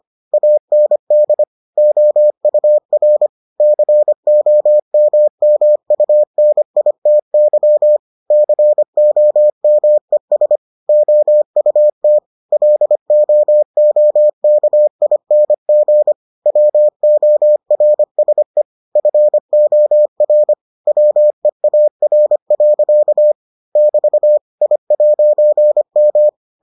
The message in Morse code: AND OUR COMMUNITY COMES OUT LOOKING WORSE FOR WEAR. <BT> I'M

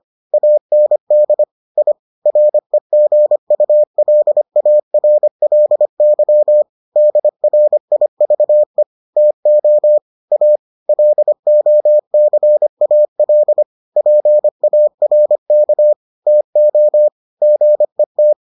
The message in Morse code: AND I REGULARLY DRIVE TO A LOCAL PARK TO GET

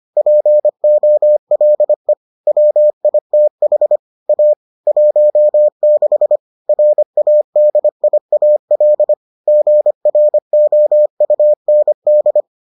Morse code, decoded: POLE WITH A 16 RADIAL GROUND

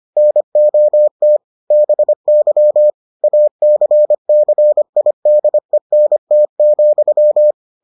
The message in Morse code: NOT BY ACCIDENT,

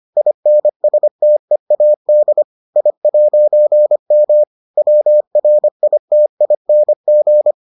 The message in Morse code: INSTEAD I'M WRITING

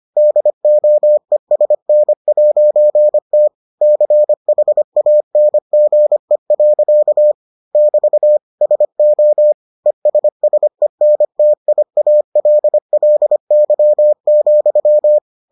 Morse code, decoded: DOESN'T CHANGE. <BT> SO ESSENTIALLY,

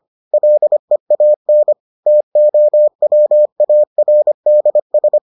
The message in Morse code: LEAN TOWARDS

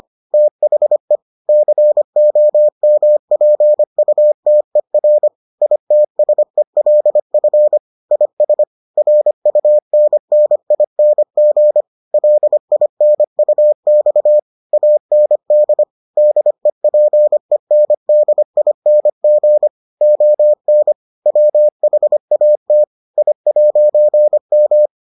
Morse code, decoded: THE COMPUTER ITSELF IS RUNNING LINUX AND DEPENDING ON WHAT I'M

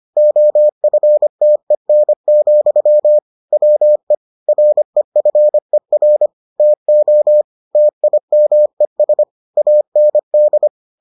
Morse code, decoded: OFTEN, WE REFER TO TIMES AND

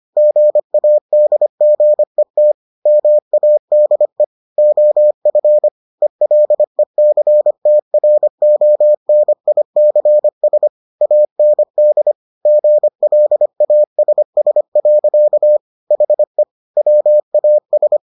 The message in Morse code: GADGET MADE OF ELECTRONICS AND GLASS. HE WAS

